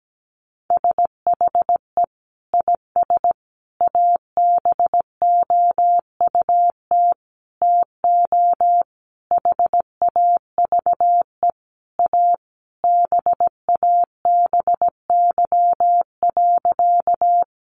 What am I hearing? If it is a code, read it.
Morse code, heard as a keyed tone, about 17 wpm, 700 Hz. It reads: SHE IS ABOUT TO HAVE A BABY.
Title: Sentences from Top 1300 Words 17wpm